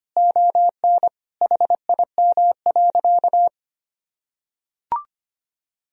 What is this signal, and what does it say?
Morse code, heard as a keyed tone, about 25 wpm, 700 Hz. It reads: ON HIM.
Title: Sentences from Top 1000 Words - Review Part 1 - 25wpm